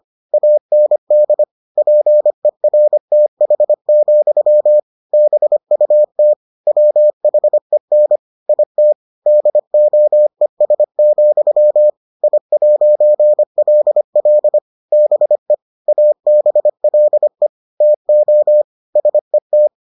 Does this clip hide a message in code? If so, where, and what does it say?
Yes: Morse code from 0.3 to 19.7 s, heard as a keyed tone, about 25 wpm, 600 Hz. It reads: AND PERTH, BUT WHEN IT DOES, I'LL BE ABLE TO SET